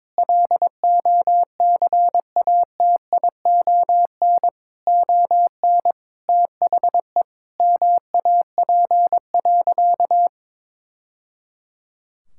Morse code, heard as a keyed tone, about 22 wpm, 700 Hz. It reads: LOCATION ON THE MAP.
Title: Sentences from Top 1000 Words - Review Part 6 - 22wpm